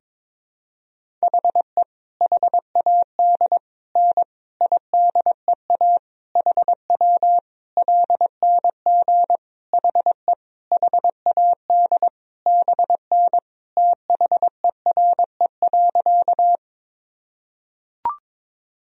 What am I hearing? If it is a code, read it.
Morse code, heard as a keyed tone, about 22 wpm, 700 Hz. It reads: HE HAD N IDEA HW LNG HE HAD BN THERE.
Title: Sentences from Top 300 Words and Abbreviations 22wpm